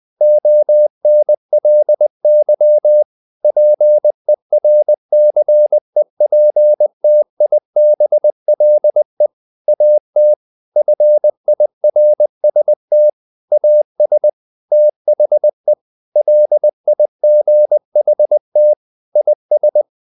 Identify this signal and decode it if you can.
Morse code, heard as a keyed tone, about 20 wpm, 600 Hz. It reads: ONLY PERCEPTIBLE AT FIRST AS THE LIGHT IS
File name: the old man and the sea part 2 20wpm